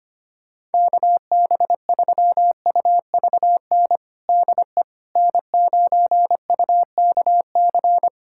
Morse code, heard as a keyed tone, about 25 wpm, 700 Hz. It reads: KB3UVN DE N9UKC
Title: USA Callsign DE Callsign 25wpm